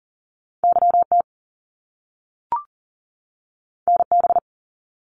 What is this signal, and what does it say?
Morse code, heard as a keyed tone, about 40 wpm, 700 Hz. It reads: YT NB
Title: Canadian Provinces and Territories 40wpm